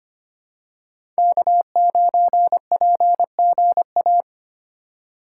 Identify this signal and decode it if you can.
Morse code, heard as a keyed tone, about 25 wpm, 700 Hz. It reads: K9PGA
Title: USA Call Signs - 3x Character Spacing - 25wpm